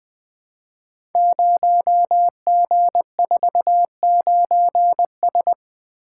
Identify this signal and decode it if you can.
Morse code, heard as a keyed tone, about 20 wpm, 700 Hz. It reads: 0G49S
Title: Code Groups - 5 Letter:Numbers - 20wpm